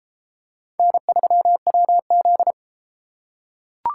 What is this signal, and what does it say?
Morse code, heard as a keyed tone, about 30 wpm, 700 Hz. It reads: N3WZ